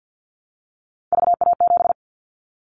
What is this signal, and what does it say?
Morse code, heard as a keyed tone, about 50 wpm, 700 Hz. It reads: VA7
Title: Callsign Prefixes for Top 25 DXCC - 50wpm